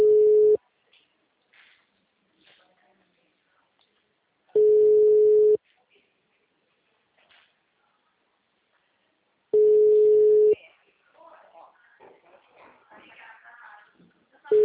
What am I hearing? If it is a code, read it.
Ringback tone (Europe, 425 Hz), with speech.